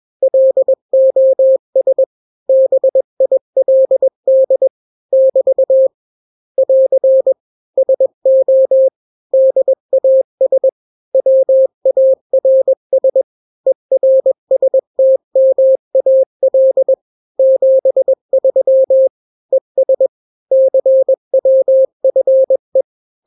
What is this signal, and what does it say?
Morse code, heard as a keyed tone, about 21 wpm, 530 Hz. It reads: LOS BILD <BT> <AR> SO DAS WARS ERSTMAL 73 ES CWFE
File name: daily / 20wpm